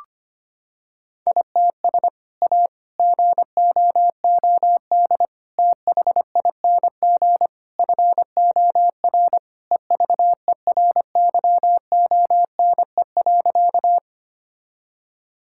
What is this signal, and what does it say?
Morse code, heard as a keyed tone, about 25 wpm, 700 Hz. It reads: ITS A GOOD THING FOR EVERYONE.
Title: Sentences from Top 1000 Words - Review Part 5 - 25wpm